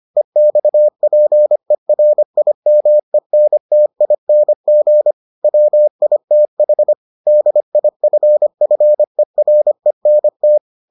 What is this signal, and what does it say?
Morse code, heard as a keyed tone, about 25 wpm, 600 Hz. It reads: EXPERIMENTING WITH DIFFERENT